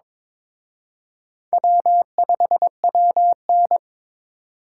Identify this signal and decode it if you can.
Morse code, heard as a keyed tone, about 26 wpm, 700 Hz. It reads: W5WN
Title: Speed Racing - USA 1x2 Call Signs 22wpm